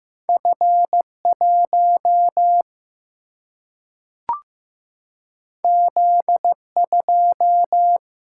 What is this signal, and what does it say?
Morse code, heard as a keyed tone, about 15 wpm, 700 Hz. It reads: F1 Z2
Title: Single-Letter Single-Number 15wpm